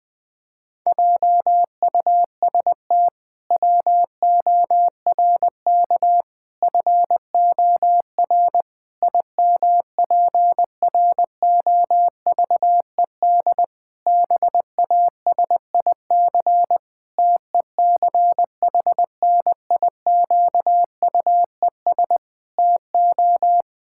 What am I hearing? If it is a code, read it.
Morse code, heard as a keyed tone, about 20 wpm, 700 Hz. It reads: JUST WORK FOR IMPROVED BASIC TECHNIQUES TO
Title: Pangram Challenge v2 - 20wpm